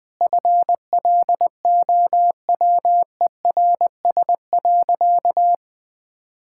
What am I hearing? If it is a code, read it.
Morse code, heard as a keyed tone, about 21 wpm, 700 Hz. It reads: FLOWERS.
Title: Easy Sentences with Hints - 20wpm